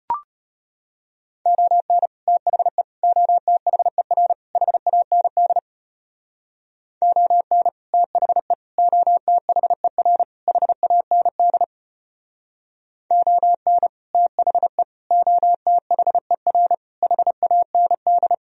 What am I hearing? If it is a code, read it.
Morse code, heard as a keyed tone, about 33 wpm, 700 Hz. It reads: ON THE OTHER HAND ON THE OTHER HAND ON THE OTHER HAND